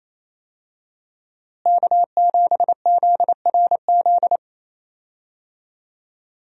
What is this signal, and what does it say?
Morse code, heard as a keyed tone, about 28 wpm, 700 Hz. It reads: K7ZRZ
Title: USA Call Signs - 5 Characters - Repeated 2x - 28wpm